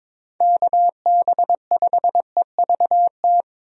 Morse code, heard as a keyed tone, about 22 wpm, 700 Hz. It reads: KB5EVT
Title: USA Call Signs - 6 Characters - Repeated 3x - 22wpm